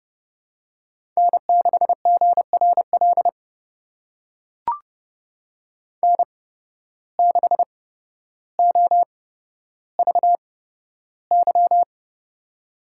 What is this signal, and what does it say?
Morse code, heard as a keyed tone, about 29 wpm, spaced as for 13 wpm, 700 Hz. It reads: N6GRL N6OVY
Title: USA Call Signs - 8x Character Spacing - 30wpm